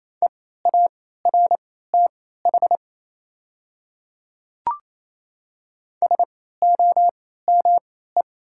Morse code, heard as a keyed tone, about 28 wpm, spaced as for 16 wpm, 690 Hz. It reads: EARTH SOME
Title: 3x Character Spacing - Top 200 Words - 28wpm